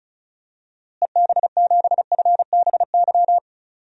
Morse code, heard as a keyed tone, about 36 wpm, 700 Hz. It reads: EB7FBY
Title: DXCC-100 Call Signs - 6 Characters - Repeated 3x - 35wpm